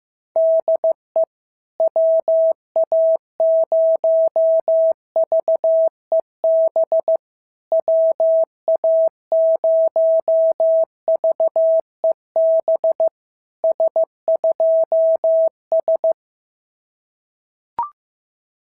Morse code, heard as a keyed tone, about 15 wpm, 650 Hz. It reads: DE WA0VEB WA0VEB S2S
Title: SOTA QSO Practice 15wpm